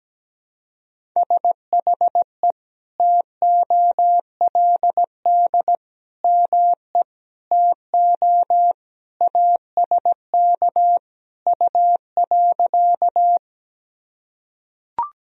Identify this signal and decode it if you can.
Morse code, heard as a keyed tone, about 17 wpm, 700 Hz. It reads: SHE TOLD ME TO ASK U.
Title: Sentences from Top 400 Words and Abbreviations 17wpm